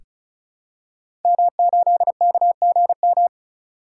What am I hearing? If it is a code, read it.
Morse code, heard as a keyed tone, about 36 wpm, 700 Hz. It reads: M8KGM
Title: Code Groups - 5 Letter:Numbers - 35wpm